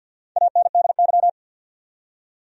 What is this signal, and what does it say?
Morse code, heard as a keyed tone, about 53 wpm, 700 Hz. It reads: ANDY
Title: USA Common Names - Max 5 Letters - 50wpm